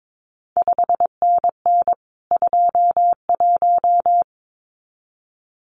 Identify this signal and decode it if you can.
Morse code, heard as a keyed tone, about 22 wpm, 700 Hz. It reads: 5NN 21
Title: CQ WW DX Contest Exchange Encore - 22wpm